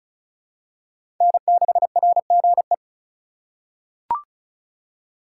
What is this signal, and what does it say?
Morse code, heard as a keyed tone, about 34 wpm, 700 Hz. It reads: N6RGE
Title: USA Call Signs - 5 Characters - 2x Character Spacing - 35wpm